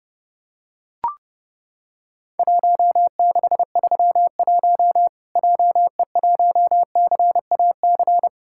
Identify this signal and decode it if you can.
Morse code, heard as a keyed tone, about 30 wpm, 700 Hz. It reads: 1631 JE1CAC
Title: Worked All Europe DX Contest - QTC Exchange - 30wpm